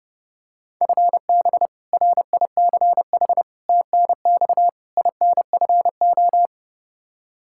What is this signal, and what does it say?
Morse code, heard as a keyed tone, about 30 wpm, 700 Hz. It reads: FB RICH TNX INFO
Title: Practice QSO - Exchange 3 - More Info - 30wpm